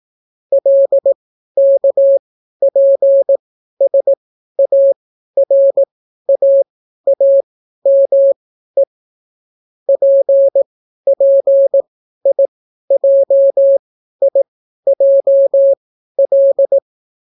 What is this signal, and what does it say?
Morse code, heard as a keyed tone, about 18 wpm, spaced as for 12 wpm, 550 Hz. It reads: LKPSARAAME PPIJIJL